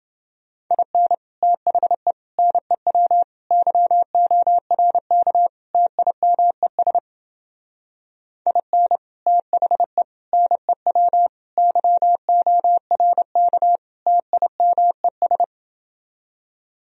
Morse code, heard as a keyed tone, about 28 wpm, 700 Hz. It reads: IN THE NEW YORK TIMES IN THE NEW YORK TIMES